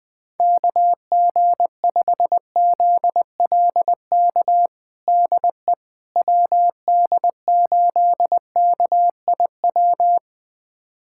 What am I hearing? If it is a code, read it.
Morse code, heard as a keyed tone, about 20 wpm, 700 Hz. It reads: KG5ZLK DE WD8KIW
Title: USA Callsign DE Callsign - 6 Characters - 20wpm